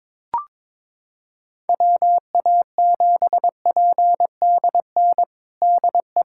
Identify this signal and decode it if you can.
Morse code, heard as a keyed tone, about 22 wpm, 700 Hz. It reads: WA7PDN DE
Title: USA Callsign DE Callsign 22wpm